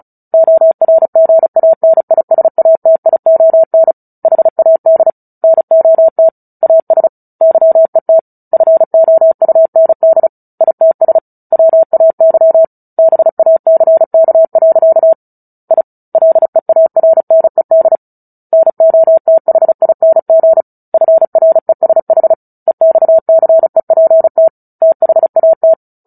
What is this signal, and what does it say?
Morse code, heard as a keyed tone, about 34 wpm, 650 Hz. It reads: ORGANISATION HAD NOT AS YET FOUND ITS WAY BACK. I LEARNED NOTHING FRESH EXCEPT THAT
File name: Book Two / 09-Chapter 9 - 35 WPM